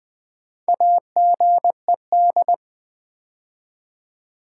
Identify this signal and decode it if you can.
Morse code, heard as a keyed tone, about 20 wpm, 700 Hz. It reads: AGED
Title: Build A Word - 2 Letters At A Time - 4-Letter Words - 20wpm